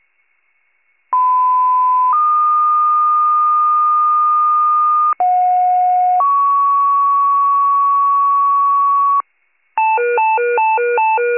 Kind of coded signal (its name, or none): two-tone page